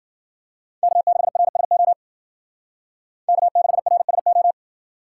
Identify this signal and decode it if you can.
Morse code, heard as a keyed tone, about 50 wpm, 700 Hz. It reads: K6RSO K6RSO
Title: Speed Racing - USA 1x3 Call Signs 40wpm